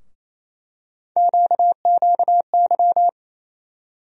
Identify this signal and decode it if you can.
Morse code, heard as a keyed tone, about 28 wpm, 700 Hz. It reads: QQY
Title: USA Call Sign Postfixes - 3 Letters 28wpm